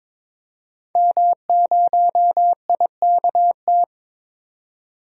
Morse code, heard as a keyed tone, about 22 wpm, 700 Hz. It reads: M0IKT
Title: DXCC-100 Call Signs - 5 Characters - 22wpm